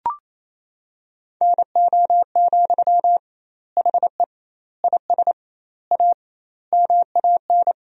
Morse code, heard as a keyed tone, about 28 wpm, 700 Hz. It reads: NO, HE IS A MAN
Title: 2x Word-Spacing - Sentences from Top 400 Words - 28wpm